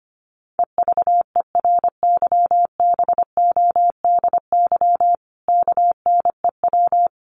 Morse code, heard as a keyed tone, about 25 wpm, 700 Hz. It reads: EVERYBODY KNEW
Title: Sentences from Top 1400 Words 25wpm